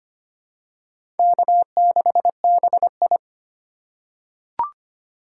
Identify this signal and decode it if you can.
Morse code, heard as a keyed tone, about 25 wpm, 700 Hz. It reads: K6BI
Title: USA Call Signs - 3x Character Spacing - 25wpm